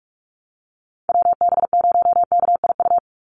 Morse code, heard as a keyed tone, about 44 wpm, 690 Hz. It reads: WB0KIU
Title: NA SOTA Chaser Callsigns 45wpm